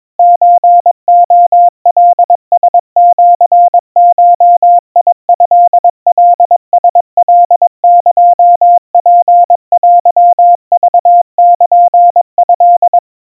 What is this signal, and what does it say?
Morse code (Japanese Wabun code), heard as a keyed tone, about 22 wpm, 700 Hz. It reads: ソレカラシゴトカラカエツテクルト
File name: 80 cpm 700 Hz Crispy